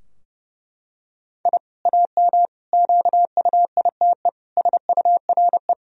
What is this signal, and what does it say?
Morse code, heard as a keyed tone, about 32 wpm, 700 Hz. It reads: I AM QUITE SURE
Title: Sentences from Top 1000 Words - Review Part 2 - 30wpm